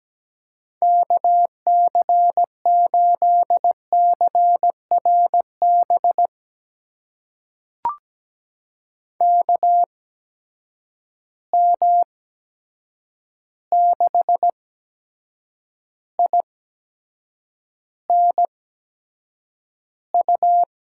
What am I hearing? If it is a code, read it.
Morse code, heard as a keyed tone, about 17 wpm, spaced as for 7 wpm, 700 Hz. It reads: KC8CRB KM6INU
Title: USA Call Signs - 8x Character Spacing - 17wpm